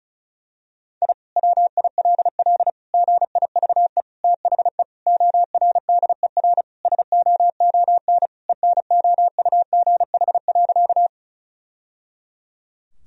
Morse code, heard as a keyed tone, about 34 wpm, 700 Hz. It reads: I WILL GIVE THE ORDER SOON ENOUGH.
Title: Sentences from Top 1000 Words - Review Part 2 - 35wpm